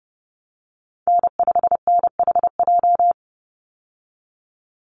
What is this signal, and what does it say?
Morse code, heard as a keyed tone, about 29 wpm, 700 Hz. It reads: N5NHJ
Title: Call Signs - 30wpm